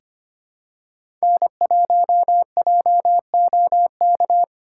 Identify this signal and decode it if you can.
Morse code, heard as a keyed tone, about 25 wpm, 700 Hz. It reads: N1JOK
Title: USA Call Signs - 5 Characters - 2x Character Spacing - 25wpm